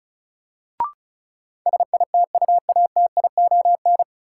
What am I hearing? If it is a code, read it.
Morse code, heard as a keyed tone, about 36 wpm, 700 Hz. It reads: SITUATION
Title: Sentences from Top 1200 Words - New Word Drill Alt - 35wpm